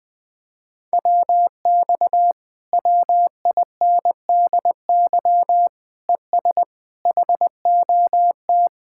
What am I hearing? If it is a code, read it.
Morse code, heard as a keyed tone, about 20 wpm, 700 Hz. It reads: WX WINDY ES HOT
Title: Speed-Racing Common QSO Phrases - 17wpm